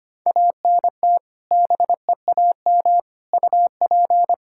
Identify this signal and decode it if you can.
Morse code, heard as a keyed tone, about 25 wpm, 700 Hz. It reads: ANT BEAM UP
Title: K1USN SST QSO Practice 25wpm